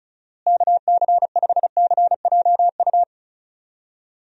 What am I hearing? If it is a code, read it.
Morse code, heard as a keyed tone, about 36 wpm, 700 Hz. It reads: KC5CJU
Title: USA Call Signs - 6 Characters - Repeated 3x - 35wpm